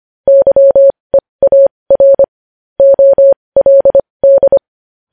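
Morse code, heard as a keyed tone, about 25 wpm, spaced as for 20 wpm, 550 Hz. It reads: YEAR OLD
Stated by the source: Morse code news headlines practice at 20.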